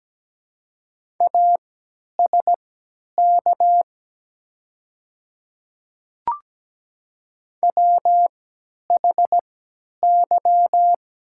Morse code, heard as a keyed tone, about 17 wpm, spaced as for 10 wpm, 700 Hz. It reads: ASK WHY
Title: 3x Character Spacing - Top 200 Words - 17wpm